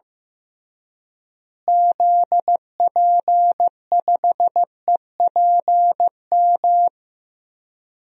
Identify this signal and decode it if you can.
Morse code, heard as a keyed tone, about 15 wpm, 700 Hz. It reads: ZP5EPM
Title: DXCC-100 Call Signs - 6 Characters - Repeated 2x - 15wpm